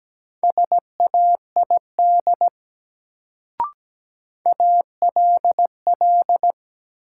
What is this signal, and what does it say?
Morse code, heard as a keyed tone, about 17 wpm, 700 Hz. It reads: SAID ALL
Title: Sentences from Top 100 Words - New Word Drill - 17wpm